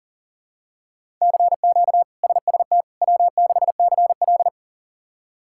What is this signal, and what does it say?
Morse code, heard as a keyed tone, about 40 wpm, 700 Hz. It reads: CQ SST W6CL
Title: K1USN SST QSO Practice - 40wpm